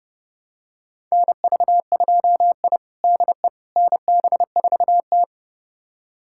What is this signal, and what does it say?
Morse code, heard as a keyed tone, about 32 wpm, 700 Hz. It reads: NV2I DE NB4T
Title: USA Callsign DE Callsign - 4 Characters - 30wpm